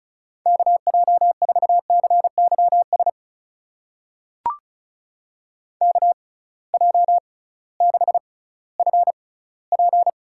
Morse code, heard as a keyed tone, about 36 wpm, spaced as for 19 wpm, 700 Hz. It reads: KJ4CYS KJ6FP
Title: USA Call Signs - 6 Characters - 6x Character Spacing - 35wpm